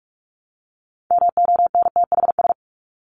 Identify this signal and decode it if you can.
Morse code, heard as a keyed tone, about 44 wpm, 700 Hz. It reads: MONTHS
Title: ES:S Suffix - Top 100 Words with Base Words 45wpm